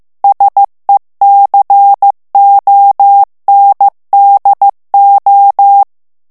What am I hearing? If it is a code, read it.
Morse code, heard as a keyed tone, about 15 wpm, 800 Hz. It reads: SECONDO
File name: morse mp3 / ITALIEN texte clair 2 vitesse 14 mots minute